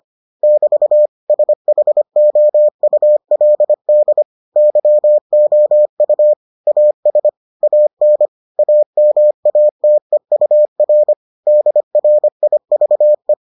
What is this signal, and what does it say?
Morse code, heard as a keyed tone, about 25 wpm, 600 Hz. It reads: <BT> SHOULD YOU AS AN AMATEUR DRIVE